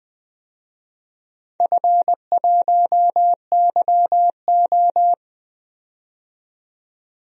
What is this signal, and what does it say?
Morse code, heard as a keyed tone, about 20 wpm, 700 Hz. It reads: F1YO
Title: DXCC-100 Call Signs - 4 Characters - 20wpm